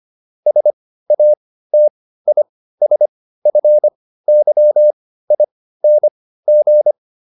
Morse code, heard as a keyed tone, about 25 wpm, spaced as for 15 wpm, 600 Hz. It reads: SATISFYING